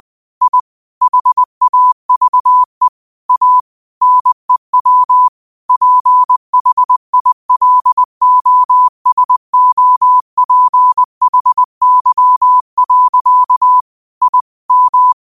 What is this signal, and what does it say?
Morse code, heard as a keyed tone, about 20 wpm, 1000 Hz. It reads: I HAVE A NEW PHILOSOPHY. I M